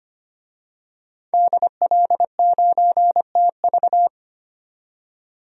Morse code, heard as a keyed tone, about 25 wpm, 700 Hz. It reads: DL9TV
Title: DXCC-100 Call Signs - 5 Characters - 25wpm